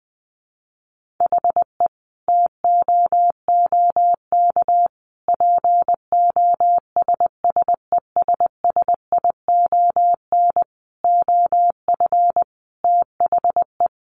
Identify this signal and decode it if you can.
Morse code, heard as a keyed tone, about 20 wpm, 700 Hz. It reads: HE TOOK POSSESSION OF THE
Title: Sentences from Top 1500 Words 20wpm